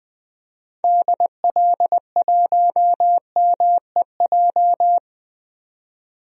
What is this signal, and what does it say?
Morse code, heard as a keyed tone, about 20 wpm, 700 Hz. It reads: DL1MEJ
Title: DXCC-100 Call Signs - 6 Characters - 20wpm